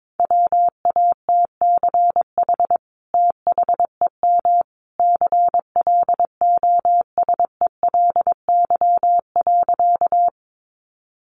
Morse code, heard as a keyed tone, about 22 wpm, 700 Hz. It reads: WATCH THEM CLOSELY.
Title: Sentences from Top 1400 Words 22wpm